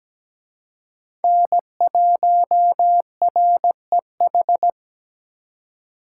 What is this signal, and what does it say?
Morse code, heard as a keyed tone, about 20 wpm, 700 Hz. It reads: N1REH
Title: Speed Racing - USA 1x3 Call Signs 17wpm